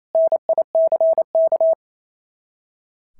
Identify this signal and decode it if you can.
Morse code, heard as a keyed tone, about 27 wpm, 690 Hz. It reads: NICK
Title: SOTA QSO Practice 28wpm